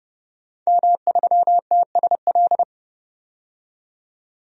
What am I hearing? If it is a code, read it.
Morse code, heard as a keyed tone, about 30 wpm, 700 Hz. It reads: M3TSL
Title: DXCC-100 Call Signs - 5 Characters - 30wpm